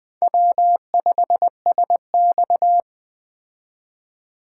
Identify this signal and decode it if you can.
Morse code, heard as a keyed tone, about 20 wpm, 700 Hz. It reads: W5SX
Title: Speed Racing - USA 1x2 Call Signs 17wpm